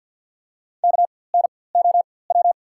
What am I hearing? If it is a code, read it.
Morse code, heard as a keyed tone, about 52 wpm, spaced as for 23 wpm, 690 Hz. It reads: KNOW
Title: 4x Character Spacing - Top 500 Words - 50wpm